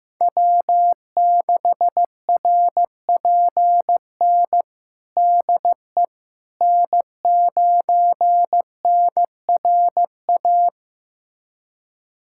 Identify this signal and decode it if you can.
Morse code, heard as a keyed tone, about 16 wpm, 700 Hz. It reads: W6RPN DE N9NRA
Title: USA Callsign DE Callsign - 5 Characters - 15wpm